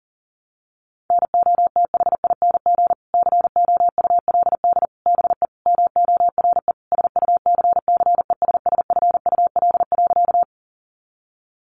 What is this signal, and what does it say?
Morse code, heard as a keyed tone, about 40 wpm, 700 Hz. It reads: NOTHING COULD BE MORE SUCCESSFUL.
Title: Sentences from Top 900 Words 40wpm